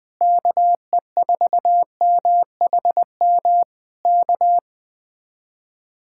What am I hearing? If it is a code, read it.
Morse code, heard as a keyed tone, about 20 wpm, 700 Hz. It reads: KE4MHM K